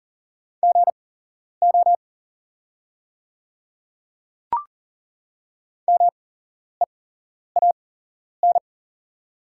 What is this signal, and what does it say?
Morse code, heard as a keyed tone, about 40 wpm, spaced as for 11 wpm, 700 Hz. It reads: GO MEAN
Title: 8x Character Spacing - Top 200 Words - 40wpm